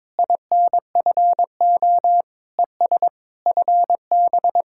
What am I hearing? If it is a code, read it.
Morse code, heard as a keyed tone, about 22 wpm, 700 Hz. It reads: INFO ES FB
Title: Speed-Racing Common QSO Phrases - 22wpm